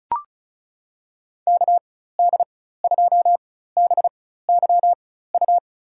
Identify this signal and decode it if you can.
Morse code, heard as a keyed tone, about 35 wpm, spaced as for 16 wpm, 700 Hz. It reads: KD2BYU
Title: USA Call Signs - 4x Character Spacing - 35wpm